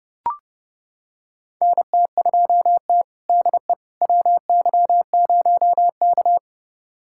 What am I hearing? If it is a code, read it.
Morse code, heard as a keyed tone, about 32 wpm, 700 Hz. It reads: NT2T DE WY0K
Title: USA Callsign DE Callsign - 4 Characters - 30wpm